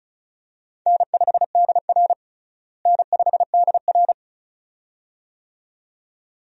Morse code, heard as a keyed tone, about 36 wpm, 700 Hz. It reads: N5DR N5DR
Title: USA Call Signs - 4 Characters - Repeated 2x - 35wpm